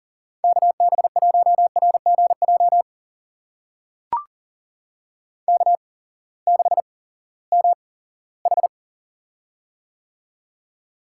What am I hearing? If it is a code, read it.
Morse code, heard as a keyed tone, about 39 wpm, spaced as for 20 wpm, 700 Hz. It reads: KB1RGJ K6MH
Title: USA Call Signs - 8x Character Spacing - 40wpm